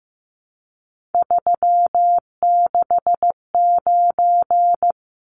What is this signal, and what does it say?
Morse code, heard as a keyed tone, about 15 wpm, 700 Hz. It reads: 369
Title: Numbers - 3-Digit 15wpm